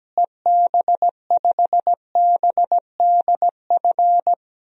Morse code, heard as a keyed tone, about 17 wpm, 700 Hz. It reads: EB5BDF
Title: DXCC-100 Call Signs - 6 Characters - Repeated 3x - 17wpm